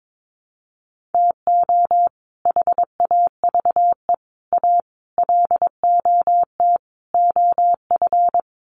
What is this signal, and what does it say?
Morse code, heard as a keyed tone, about 22 wpm, 700 Hz. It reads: TO HAVE A LOT OF
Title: Sets of 5 Words Encore 22wpm